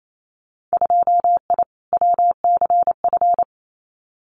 Morse code, heard as a keyed tone, about 28 wpm, 700 Hz. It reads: 2I WCF
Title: Winter Field Day Exchanges 28wpm